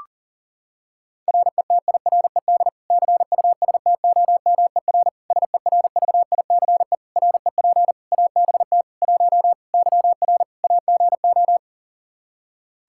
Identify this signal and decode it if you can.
Morse code, heard as a keyed tone, about 40 wpm, 700 Hz. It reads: RETIRED CUSTOMER SERVICE REP ABT 1 YR AGO
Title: Practice QSO - Exchange 3 - More Info - 40wpm